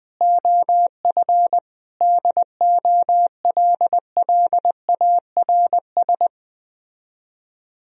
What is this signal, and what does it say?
Morse code, heard as a keyed tone, about 20 wpm, 700 Hz. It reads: OF DOLLARS